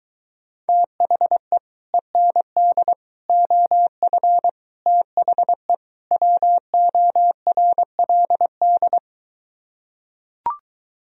Morse code, heard as a keyed tone, about 24 wpm, 700 Hz. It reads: THE END OF THE WORLD